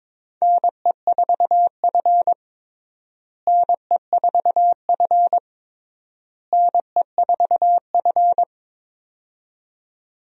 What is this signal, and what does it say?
Morse code, heard as a keyed tone, about 21 wpm, 700 Hz. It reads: NE4F NE4F NE4F
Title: USA Call Signs - 4 Characters - Repeated 3x - 22wpm